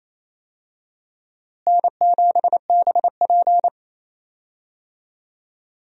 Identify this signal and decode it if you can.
Morse code, heard as a keyed tone, about 28 wpm, 700 Hz. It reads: N7BP
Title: USA Call Signs - 4 Characters - Repeated 2x - 28wpm